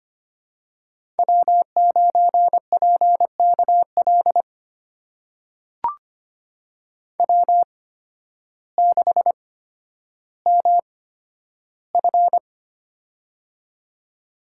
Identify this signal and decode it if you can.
Morse code, heard as a keyed tone, about 25 wpm, spaced as for 12 wpm, 700 Hz. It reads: W9PKL W6MF
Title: USA Call Signs - 8x Character Spacing - 25wpm